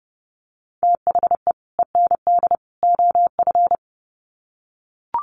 Morse code, heard as a keyed tone, about 30 wpm, 700 Hz. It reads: THE END OF
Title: Sets of 3 Words 30wpm